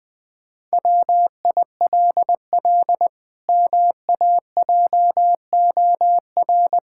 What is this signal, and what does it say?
Morse code, heard as a keyed tone, about 21 wpm, 700 Hz. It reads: WILL MAJOR
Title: Pangram Challenge v2 - 20wpm